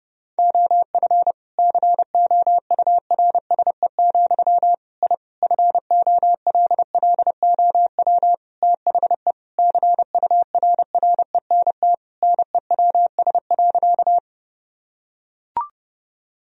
Morse code, heard as a keyed tone, about 28 wpm, 700 Hz. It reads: OF COURSE, I FOLLOW THE CURRENT NEWS.
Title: Sentences from Top 1000 Words - Review Part 4 - 30wpm